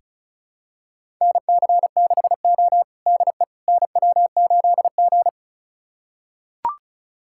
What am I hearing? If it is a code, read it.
Morse code, heard as a keyed tone, about 36 wpm, 700 Hz. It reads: NC6O DE NW8G
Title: USA Callsign DE Callsign - 4 Characters - 35wpm